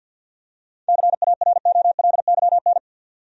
Morse code, heard as a keyed tone, about 51 wpm, 700 Hz. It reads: CAROLYN
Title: USA Common Names - 50wpm